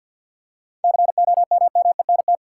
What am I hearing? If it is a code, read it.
Morse code, heard as a keyed tone, about 53 wpm, 700 Hz. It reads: COMMENT
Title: Seven Letter Words - 50wpm